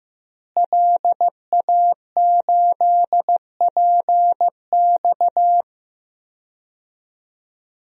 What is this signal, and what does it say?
Morse code, heard as a keyed tone, about 15 wpm, 700 Hz. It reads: LA8PX
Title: DXCC-100 Call Signs - 5 Characters - Repeated 3x - 15wpm